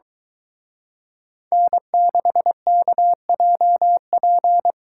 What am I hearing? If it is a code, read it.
Morse code, heard as a keyed tone, about 23 wpm, 700 Hz. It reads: N6KJP